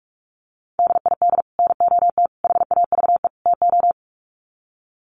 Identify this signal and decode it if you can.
Morse code, heard as a keyed tone, about 44 wpm, 700 Hz. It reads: DID NOT HAVE TO
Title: Sets of 4 Words 45wpm